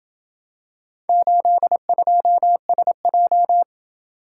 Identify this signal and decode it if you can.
Morse code, heard as a keyed tone, about 27 wpm, 700 Hz. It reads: 82SJ